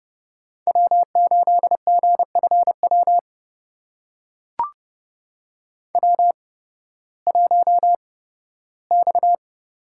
Morse code, heard as a keyed tone, about 30 wpm, spaced as for 16 wpm, 700 Hz. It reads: W8GFW W1X
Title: USA Call Signs - 5 Characters - 8x Character Spacing - 30wpm